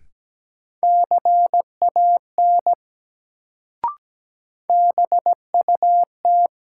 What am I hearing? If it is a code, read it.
Morse code, heard as a keyed tone, about 17 wpm, 700 Hz. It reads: CAN BUT
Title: Sentences from Top 100 Words - New Word Drill - 17wpm